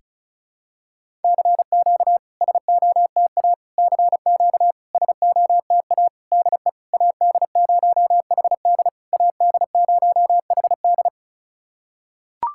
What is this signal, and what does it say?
Morse code, heard as a keyed tone, about 36 wpm, 690 Hz. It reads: CQ SOTA CQ SOTA DE AD0HD AD0HD
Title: SOTA QSO Practice 35wpm